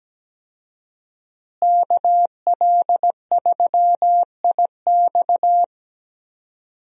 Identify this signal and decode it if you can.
Morse code, heard as a keyed tone, about 17 wpm, 700 Hz. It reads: KL3IX
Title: USA Call Signs - 5 Characters - Repeated 3x - 17wpm